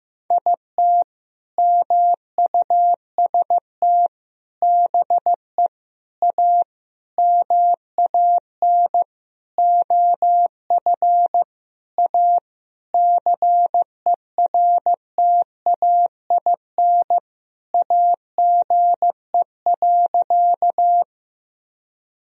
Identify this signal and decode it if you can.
Morse code, heard as a keyed tone, about 15 wpm, 700 Hz. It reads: IT MUST BE A MAN OF A CERTAIN AGE.
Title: Sentences from Top 1000 Words - Review Part 3 - 15wpm